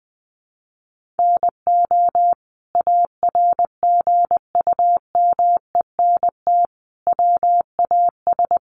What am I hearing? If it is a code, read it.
Morse code, heard as a keyed tone, about 20 wpm, 700 Hz. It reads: NO ARGUMENT WAS
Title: Sentences from Top 1400 Words 20wpm